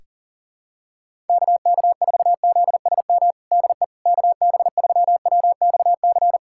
Morse code, heard as a keyed tone, about 40 wpm, 700 Hz. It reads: KK4ZSM DE KB3WXC
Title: USA Callsign DE Callsign - 6 Characters - 40wpm